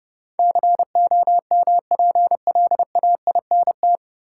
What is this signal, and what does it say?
Morse code, heard as a keyed tone, about 30 wpm, 700 Hz. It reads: COMPLAINT
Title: Sentences from Top 1400 Words - New Word Drill - 30wpm